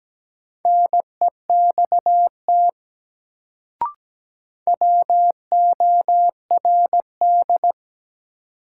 Morse code, heard as a keyed tone, about 17 wpm, 700 Hz. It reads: NEXT WORD
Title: Sentences from Top 300 Words - New Word Drill - 17wpm